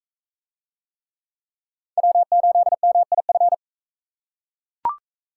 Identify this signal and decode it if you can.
Morse code, heard as a keyed tone, about 38 wpm, 700 Hz. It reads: W8MIF